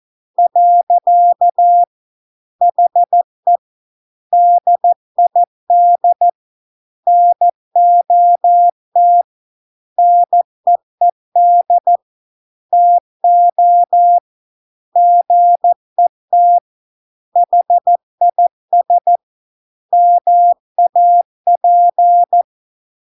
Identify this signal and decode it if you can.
Morse code, heard as a keyed tone, about 14 wpm, 700 Hz. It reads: . HE DID NOT NEED TO GET HIS MAP